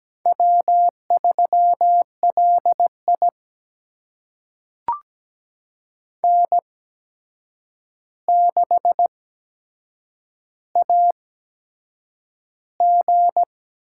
Morse code, heard as a keyed tone, about 17 wpm, spaced as for 7 wpm, 700 Hz. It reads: W3LI N6AG
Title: USA Call Signs - 8x Character Spacing - 17wpm